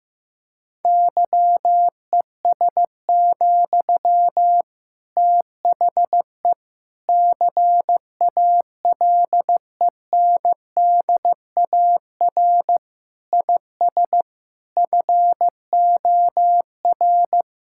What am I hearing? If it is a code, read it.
Morse code, heard as a keyed tone, about 15 wpm, 700 Hz. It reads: YES, THE CALENDAR IS FOR
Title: Sentences from Top 1600 Words 15wpm